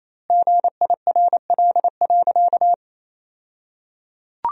Morse code, heard as a keyed tone, about 28 wpm, 700 Hz. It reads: GIRL.
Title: 2x Word-Spacing - Sentences from Top 400 Words - 28wpm